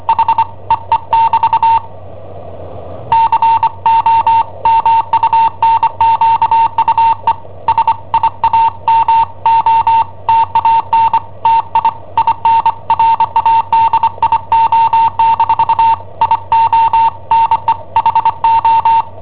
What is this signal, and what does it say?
Morse code, heard as a keyed tone, about 25 wpm, 910 Hz. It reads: H EE<BT> COMUNQUE SIAMO TANTI IN RADIO- IO NE HO
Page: Cassetta esercizi preparata da I7OHP per IW5ELP